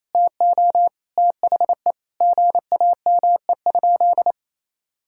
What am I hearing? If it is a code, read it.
Morse code, heard as a keyed tone, about 28 wpm, 700 Hz. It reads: TO THE GAME?
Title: Sentences from Top 1000 Words - Review Part 6 - 28wpm